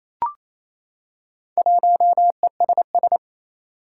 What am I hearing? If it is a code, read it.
Morse code, heard as a keyed tone, about 29 wpm, 700 Hz. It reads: 1ESS
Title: Four Challenging Letter-Number Combinations 28wpm